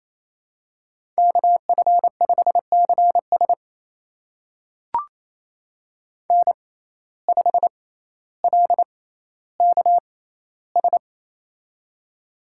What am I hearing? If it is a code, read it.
Morse code, heard as a keyed tone, about 27 wpm, spaced as for 14 wpm, 700 Hz. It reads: KF5CS N5LKS
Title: USA Call Signs - 5 Characters - 6x Character Spacing - 28wpm